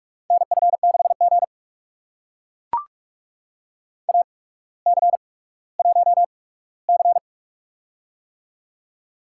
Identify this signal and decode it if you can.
Morse code, heard as a keyed tone, about 44 wpm, spaced as for 19 wpm, 690 Hz. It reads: NF6G AC1C
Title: USA Call Signs - 4 Characters - 8x Character Spacing - 45wpm